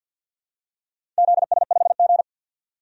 Morse code, heard as a keyed tone, about 47 wpm, 700 Hz. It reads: ZS5G
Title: DXCC-100 Call Signs - 4 Characters - 50wpm